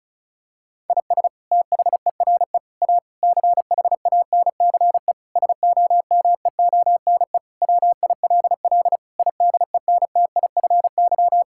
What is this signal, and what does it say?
Morse code, heard as a keyed tone, about 34 wpm, 700 Hz. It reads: IS THERE A CHANCE SOMEONE WILL IDENTIFY
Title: Sentences from Top 1000 Words - Review Part 6 - 35wpm